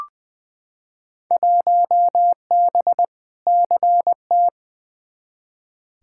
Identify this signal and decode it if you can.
Morse code, heard as a keyed tone, about 20 wpm, 700 Hz. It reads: 1B CT
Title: Field Day Exchanges Encore - 20wpm